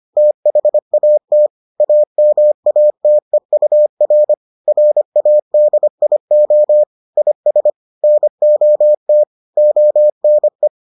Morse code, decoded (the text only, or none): THAT AMATEUR RADIO IS NOT ONE